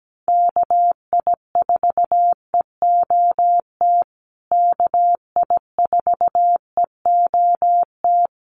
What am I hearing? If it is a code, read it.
Morse code, heard as a keyed tone, about 17 wpm, 700 Hz. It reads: KI4EOT KI4EOT
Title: USA Call Signs Repeated 3x 17wpm